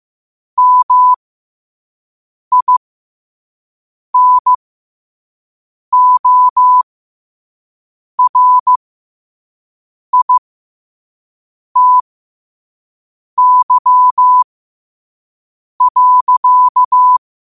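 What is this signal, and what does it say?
Morse code, heard as a keyed tone, about 15 wpm, spaced as for 5 wpm, 1000 Hz. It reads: MINORITY.
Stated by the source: Morse code sent at 5 wpm.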